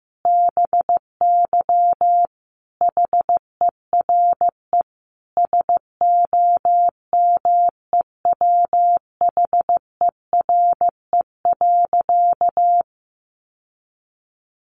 Morse code, heard as a keyed tone, about 15 wpm, 700 Hz. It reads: BY HERE SOMEWHERE.
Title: Sentences from Top 1400 Words 15wpm